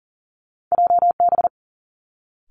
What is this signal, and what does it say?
Morse code, heard as a keyed tone, about 40 wpm, 700 Hz. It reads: JB
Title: Two Challenging Letter-Number Combinations 40wpm